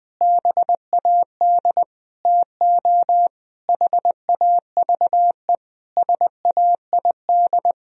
Morse code, heard as a keyed tone, about 20 wpm, 700 Hz. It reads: BAD TO HAVE SAID
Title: Sentences from Top 1000 Words - Review Part 4 - 20wpm